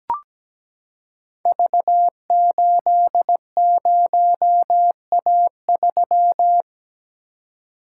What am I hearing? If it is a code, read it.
Morse code, heard as a keyed tone, about 17 wpm, 700 Hz. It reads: V80A3
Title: Code Groups - 5 Letter:Numbers - 17wpm